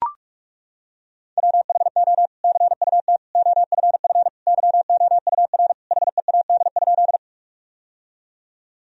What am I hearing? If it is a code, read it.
Morse code, heard as a keyed tone, about 42 wpm, 700 Hz. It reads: WHO CUT OFF YOUR HEAD?
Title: Sentences from Top 1000 Words - Review Part 1 - 45wpm